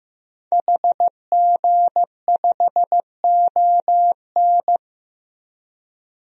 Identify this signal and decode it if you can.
Morse code, heard as a keyed tone, about 15 wpm, 700 Hz. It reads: HG5ON
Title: DXCC-100 Call Signs - 5 Characters - 15wpm